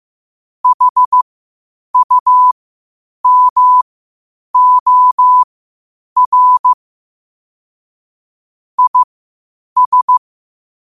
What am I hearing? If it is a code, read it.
Morse code, heard as a keyed tone, about 15 wpm, spaced as for 8 wpm, 1000 Hz. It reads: HUMOR IS
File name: qotdfiles / 8wpm